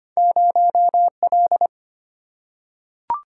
Morse code, heard as a keyed tone, about 25 wpm, 700 Hz. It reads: 0L
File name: Two Letter-Number Combinations - 25wpm